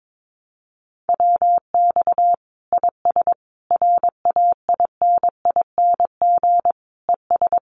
Morse code, heard as a keyed tone, about 22 wpm, 700 Hz. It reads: WX IS RAINING ES
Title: Common QSO Phrases 22wpm